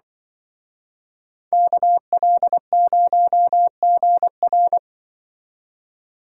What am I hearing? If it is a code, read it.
Morse code, heard as a keyed tone, about 24 wpm, 700 Hz. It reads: KL0GR
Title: Speed Racing - USA 2x2 Call Signs 20wpm